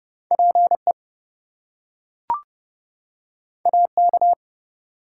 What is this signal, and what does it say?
Morse code, heard as a keyed tone, about 30 wpm, 700 Hz. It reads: PE AK
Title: US States and Canadian Provinces 30wpm